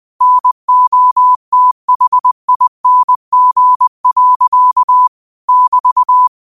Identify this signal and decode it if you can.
Morse code, heard as a keyed tone, about 20 wpm, 1000 Hz. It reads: NOTHING. <BT>
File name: qotdfiles / 20wpm